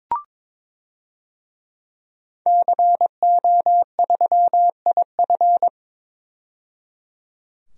Morse code, heard as a keyed tone, about 22 wpm, 700 Hz. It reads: CO3IF
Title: DXCC-100 Call Signs - 5 Characters - 22wpm